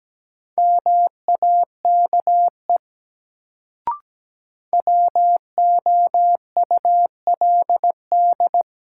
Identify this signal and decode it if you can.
Morse code, heard as a keyed tone, about 17 wpm, 700 Hz. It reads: MAKE WOULD
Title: Sentences from Top 100 Words - New Word Drill - 17wpm